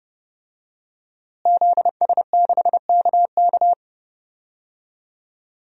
Morse code, heard as a keyed tone, about 32 wpm, 700 Hz. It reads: ZS6KK
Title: DXCC-100 Call Signs - 5 Characters - Repeated 3x - 30wpm